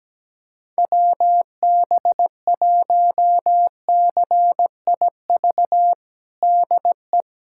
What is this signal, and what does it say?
Morse code, heard as a keyed tone, about 17 wpm, 700 Hz. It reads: WB1CIV DE
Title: USA Callsign DE Callsign - 6 Characters - 17wpm